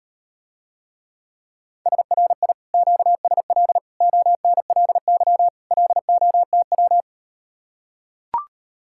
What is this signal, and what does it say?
Morse code, heard as a keyed tone, about 38 wpm, 700 Hz. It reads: SRI QSL ONLY LOTW